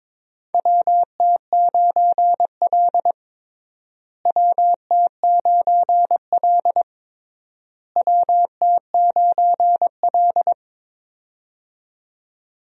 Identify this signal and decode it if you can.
Morse code, heard as a keyed tone, about 22 wpm, 700 Hz. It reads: WT9L WT9L WT9L
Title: USA Call Signs - 4 Characters - Repeated 3x - 22wpm